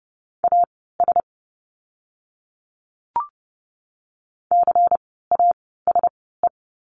Morse code, heard as a keyed tone, about 30 wpm, spaced as for 17 wpm, 700 Hz. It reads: AS CASE
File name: Lesson 17f - Letter u - Words 30wpm